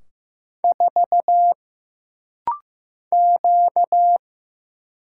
Morse code, heard as a keyed tone, about 15 wpm, 700 Hz. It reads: 4 Q
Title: Single Letter-Number Alt - Rapid-Fire 15wpm